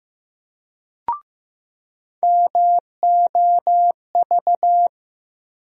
Morse code, heard as a keyed tone, about 15 wpm, 700 Hz. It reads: MOV
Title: USA Call Sign Postfixes - 3 Letters 15wpm